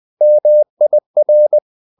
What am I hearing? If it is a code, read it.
Morse code, heard as a keyed tone, about 20 wpm, 600 Hz. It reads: MIR